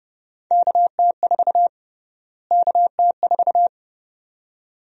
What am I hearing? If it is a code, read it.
Morse code, heard as a keyed tone, about 31 wpm, 700 Hz. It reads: KT4 KT4
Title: USA Call Sign Prefixes - Repeated 2x - 30wpm